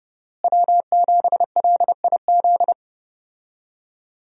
Morse code, heard as a keyed tone, about 30 wpm, 700 Hz. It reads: W7LIZ
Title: USA Call Signs - 5 Characters - Repeated 2x - 30wpm